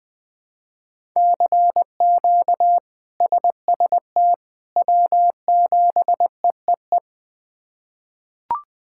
Morse code, heard as a keyed tone, about 21 wpm, 700 Hz. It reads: CQ SST W7EEE
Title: K1USN SST QSO Practice - 20wpm